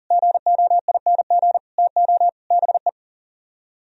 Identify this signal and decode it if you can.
Morse code, heard as a keyed tone, about 52 wpm, 700 Hz. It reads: GOING TO BE